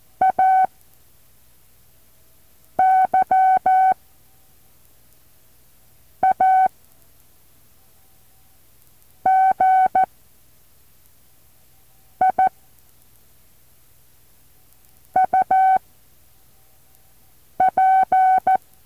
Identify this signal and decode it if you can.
Morse code, heard as a keyed tone, about 14 wpm, spaced as for 4 wpm, 760 Hz. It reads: A YAGI UP
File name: intl morse code2